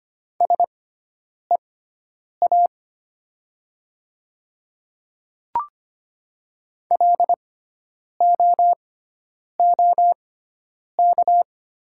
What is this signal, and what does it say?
Morse code, heard as a keyed tone, about 25 wpm, spaced as for 9 wpm, 700 Hz. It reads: SEA LOOK
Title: 6x Character Spacing - Top 300 Words - 25wpm